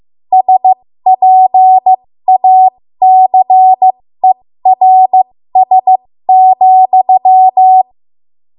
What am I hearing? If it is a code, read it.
Morse code, heard as a keyed tone, about 15 wpm, spaced as for 13 wpm, 750 Hz. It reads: SPACERS,
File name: Morse / 13WPM